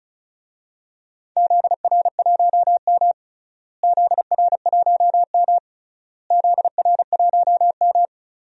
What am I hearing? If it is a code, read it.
Morse code, heard as a keyed tone, about 36 wpm, 700 Hz. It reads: ZR1M ZR1M ZR1M
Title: DXCC-100 Call Signs - 4 Characters - Repeated 3x - 35wpm